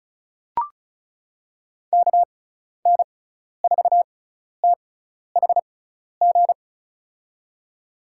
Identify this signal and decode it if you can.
Morse code, heard as a keyed tone, about 34 wpm, spaced as for 12 wpm, 690 Hz. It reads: KN4THG
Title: USA Call Signs - 6 Characters - 6x Character Spacing - 35wpm